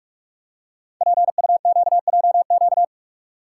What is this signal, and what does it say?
Morse code, heard as a keyed tone, about 44 wpm, 700 Hz. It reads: PUQJQ
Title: Code Groups - 5 Letters - 45wpm